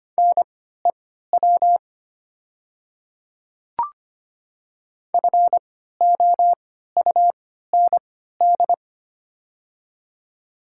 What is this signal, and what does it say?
Morse code, heard as a keyed tone, about 25 wpm, spaced as for 14 wpm, 700 Hz. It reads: NEW FOUND
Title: 3x Character Spacing - Top 500 Words - 25wpm